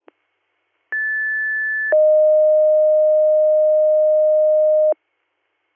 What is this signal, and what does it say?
Two-tone page, heard as a one-second tone, then a three-second tone, about 1750 then 620 Hz. It is heard from 0.9 to 4.9 s.